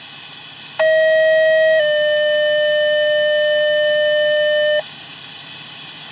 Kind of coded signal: two-tone page